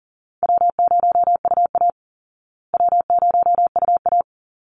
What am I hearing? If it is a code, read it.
Morse code, heard as a keyed tone, about 40 wpm, 700 Hz. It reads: W0UA W0UA
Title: Easy Call Signs Repeated 2x - 40wpm